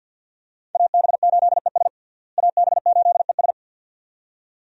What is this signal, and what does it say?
Morse code, heard as a keyed tone, about 52 wpm, 700 Hz. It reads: AB8ES AB8ES
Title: USA Call Signs - 5 Characters - Repeated 2x - 50wpm